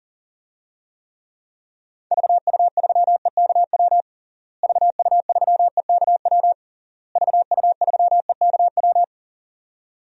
Morse code, heard as a keyed tone, about 37 wpm, 700 Hz. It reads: VU3EKW VU3EKW VU3EKW
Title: DXCC-100 Call Signs - 6 Characters - Repeated 3x - 40wpm